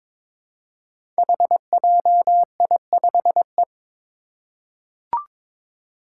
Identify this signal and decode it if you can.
Morse code, heard as a keyed tone, about 22 wpm, 700 Hz. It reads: HJI5E
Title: Five Challenging Letter-Number Combinations 22wpm